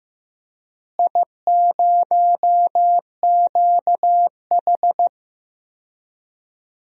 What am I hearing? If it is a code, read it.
Morse code, heard as a keyed tone, about 15 wpm, 700 Hz. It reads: I0QH
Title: DXCC-100 Call Signs - 4 Characters - Repeated 3x - 15wpm